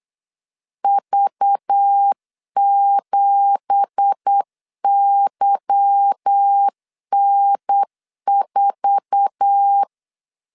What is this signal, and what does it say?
Morse code, heard as a keyed tone, about 8 wpm, 800 Hz. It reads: V7YN4
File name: CW / DET / 1 9 1500C S B 8wpm 5624